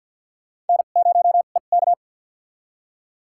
Morse code, heard as a keyed tone, about 52 wpm, spaced as for 36 wpm, 700 Hz. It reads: N0EK
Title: USA Call Signs - 4 Characters - 2x Character Spacing - 50wpm